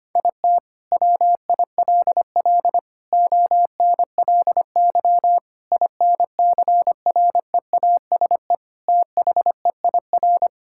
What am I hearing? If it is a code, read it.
Morse code, heard as a keyed tone, about 25 wpm, 700 Hz. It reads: IT WILL ONLY INCREASE THEIR
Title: Sentences from Top 1000 Words - Review Part 5 - 25wpm